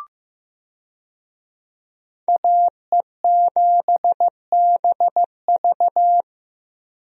Morse code, heard as a keyed tone, about 15 wpm, 700 Hz. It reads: AE7BV
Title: USA Call Signs - 5 Characters - Repeated 3x - 15wpm